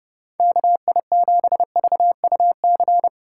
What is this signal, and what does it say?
Morse code, heard as a keyed tone, about 31 wpm, 700 Hz. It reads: KI7VUC